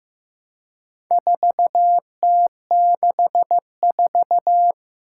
Morse code, heard as a keyed tone, about 15 wpm, 700 Hz. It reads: 4T64
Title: Four Challenging Letter-Number Combinations 15wpm